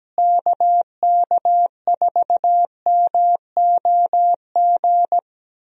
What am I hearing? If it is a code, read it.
Morse code, heard as a keyed tone, about 17 wpm, 700 Hz. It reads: KK4MOG
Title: USA Call Signs - 6 Characters - Repeated 3x - 17wpm